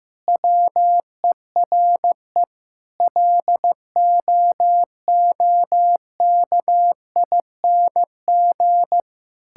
Morse code, heard as a keyed tone, about 16 wpm, 700 Hz. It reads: WERE LOOKING
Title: Sentences from Top 1000 Words - Review Part 6 - 15wpm